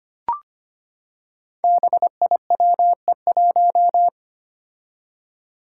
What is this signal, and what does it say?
Morse code, heard as a keyed tone, about 25 wpm, 700 Hz. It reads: BIWE1
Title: Five Challenging Letter-Number Combinations 25wpm